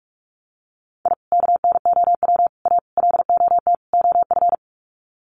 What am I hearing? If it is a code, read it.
Morse code, heard as a keyed tone, about 44 wpm, 700 Hz. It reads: I KNOW A LOT OF
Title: Sets of 5 Words Encore 45wpm